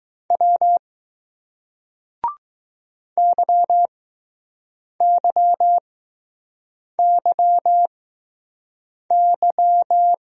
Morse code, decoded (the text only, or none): W YYYY